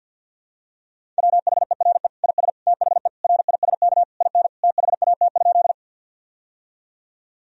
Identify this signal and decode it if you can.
Morse code, heard as a keyed tone, about 50 wpm, 700 Hz. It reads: WHERE IS THE RISK IN THAT?
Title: Sentences from Top 1000 Words - Review Part 4 - 50wpm